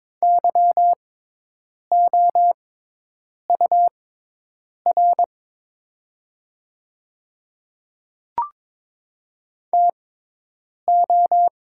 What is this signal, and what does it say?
Morse code, heard as a keyed tone, about 23 wpm, spaced as for 8 wpm, 700 Hz. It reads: YOUR TO
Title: 6x Character Spacing - Top 100 Words - 22wpm